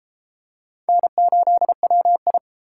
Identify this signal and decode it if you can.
Morse code, heard as a keyed tone, about 31 wpm, 700 Hz. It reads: N8WI